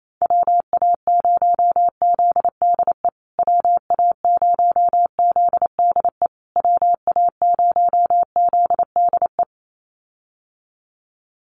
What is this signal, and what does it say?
Morse code, heard as a keyed tone, about 28 wpm, 700 Hz. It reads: WA0ZDE WA0ZDE WA0ZDE
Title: USA Call Signs Repeated 3x 28wpm